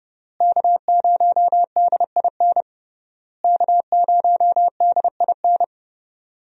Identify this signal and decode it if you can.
Morse code, heard as a keyed tone, about 31 wpm, 700 Hz. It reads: K0DIN K0DIN
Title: USA Call Signs - 5 Characters - Repeated 3x - 30wpm